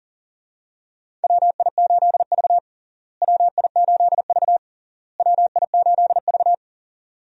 Morse code, heard as a keyed tone, about 40 wpm, 700 Hz. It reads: WI8V WI8V WI8V
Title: USA Call Signs - 4 Characters - Repeated 3x - 40wpm